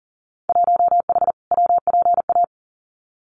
Morse code, heard as a keyed tone, about 41 wpm, 700 Hz. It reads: 1H WPA
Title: Winter Field Day Exchanges 40wpm